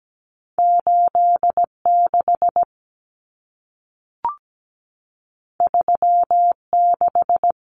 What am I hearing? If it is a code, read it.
Morse code, heard as a keyed tone, about 17 wpm, 700 Hz. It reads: 86 36
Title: Common Cut Numbers - 2-Digit 17wpm